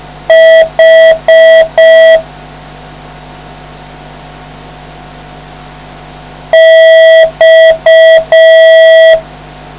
Morse code, heard as a keyed tone, about 5 wpm, 650 Hz. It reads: H X